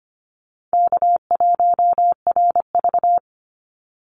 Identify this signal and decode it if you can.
Morse code, heard as a keyed tone, about 25 wpm, 700 Hz. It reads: K1RV
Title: Easy Call Signs Repeated 3x - 25wpm